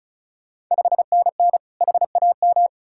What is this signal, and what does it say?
Morse code, heard as a keyed tone, about 36 wpm, 690 Hz. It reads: 5NN HAM
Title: Ontario QSO Party Exchange 35wpm